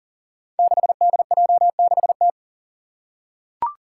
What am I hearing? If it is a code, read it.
Morse code, heard as a keyed tone, about 40 wpm, 700 Hz. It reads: 6DJ6T
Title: Five Challenging Letter-Number Combinations 40wpm